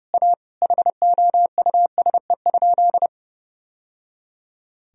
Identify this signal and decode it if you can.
Morse code, heard as a keyed tone, about 32 wpm, 700 Hz. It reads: A HOUSE?
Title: Sentences from Top 1000 Words - Review Part 1 - 30wpm